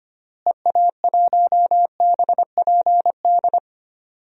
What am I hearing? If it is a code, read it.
Morse code, heard as a keyed tone, about 25 wpm, 700 Hz. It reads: EA1BPD
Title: DXCC-100 Call Signs - 6 Characters - Repeated 2x - 25wpm